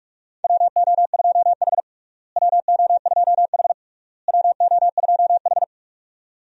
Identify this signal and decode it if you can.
Morse code, heard as a keyed tone, about 46 wpm, 700 Hz. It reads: WO2H WO2H WO2H
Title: USA Call Signs - 4 Characters - Repeated 3x - 45wpm